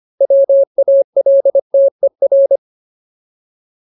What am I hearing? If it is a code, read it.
Morse code, heard as a keyed tone, about 25 wpm, 550 Hz. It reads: WALTER
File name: Ultimate Word List / 25WPM Ultimate